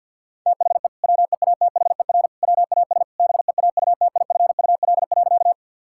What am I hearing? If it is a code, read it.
Morse code, heard as a keyed tone, about 46 wpm, 700 Hz. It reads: THE WEATHER WAS BEAUTIFUL.
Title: Sentences from Top 1000 Words - Review Part 5 - 50wpm